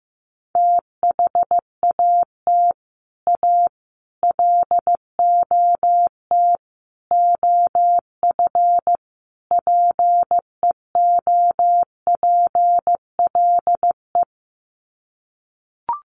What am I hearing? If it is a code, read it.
Morse code, heard as a keyed tone, about 15 wpm, 700 Hz. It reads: THAT A LOT OF PEOPLE
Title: Sets of 5 Words 15wpm